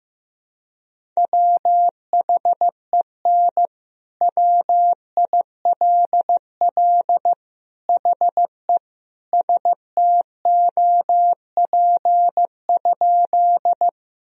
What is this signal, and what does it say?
Morse code, heard as a keyed tone, about 15 wpm, 700 Hz. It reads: WHEN WILL HE STOP?
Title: Sentences from Top 1000 Words - Review Part 1 - 15wpm